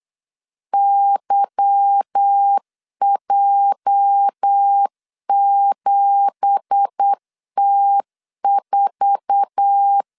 Morse code, heard as a keyed tone, about 8 wpm, 800 Hz. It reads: YJ7T4